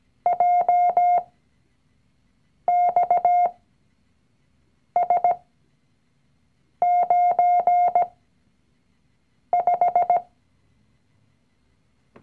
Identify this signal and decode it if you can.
Morse code, heard as a keyed tone, about 17 wpm, spaced as for 5 wpm, 710 Hz. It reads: JXS95
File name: CW-Online-Kurs / Anfänger / Lektion 019